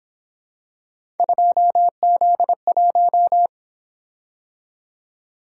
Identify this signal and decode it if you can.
Morse code, heard as a keyed tone, about 28 wpm, 700 Hz. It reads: 2Z1